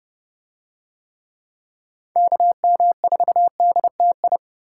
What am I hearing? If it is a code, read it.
Morse code, heard as a keyed tone, about 28 wpm, 700 Hz. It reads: KM4DTI